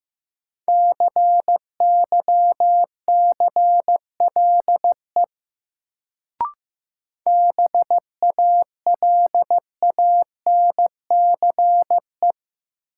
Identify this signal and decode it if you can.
Morse code, heard as a keyed tone, about 15 wpm, 700 Hz. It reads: CYCLE BALANCE
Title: Sentences from Top 900 Words - New Word Drill - 15wpm